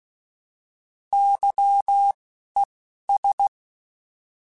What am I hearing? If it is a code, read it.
Morse code, heard as a keyed tone, about 16 wpm, 800 Hz. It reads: Y E S
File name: Alphabetical / Alphabetical mp3s / Ys @10